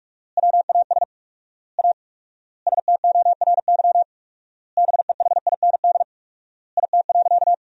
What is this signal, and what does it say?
Morse code, heard as a keyed tone, about 41 wpm, 700 Hz. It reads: WAS A STORY BEHIND IT.
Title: 4x Word-Spacing - Sentences from Top 500 Words - 45wpm